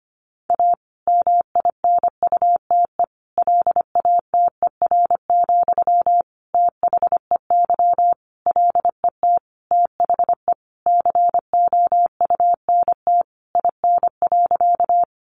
Morse code, heard as a keyed tone, about 25 wpm, 700 Hz. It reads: A MINUTE LATER, THEY LET THE COUNT IN.
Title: Sentences from Top 1500 Words 25wpm